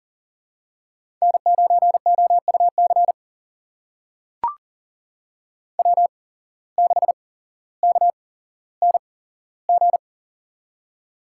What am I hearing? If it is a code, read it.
Morse code, heard as a keyed tone, about 40 wpm, spaced as for 17 wpm, 700 Hz. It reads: N9OUC W6KNG
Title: USA Call Signs - 5 Characters - 8x Character Spacing - 40wpm